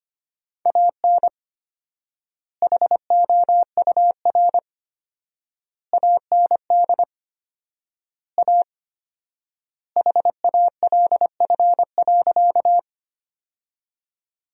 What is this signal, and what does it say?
Morse code, heard as a keyed tone, about 25 wpm, 700 Hz. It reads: AN HOUR AND A HALF.
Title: 4x Word-Spacing - Sentences from Top 400 Words - 25wpm